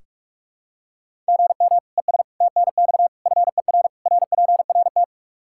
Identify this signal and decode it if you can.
Morse code, heard as a keyed tone, about 41 wpm, 700 Hz. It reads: GM ES TNX FER RPRT
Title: K1USN SST QSO Practice 45wpm